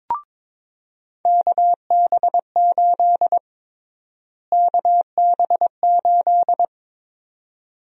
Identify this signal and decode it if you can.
Morse code, heard as a keyed tone, about 22 wpm, 700 Hz. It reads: KB8 KB8
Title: USA Call Sign Prefixes - Repeated 2x - 22wpm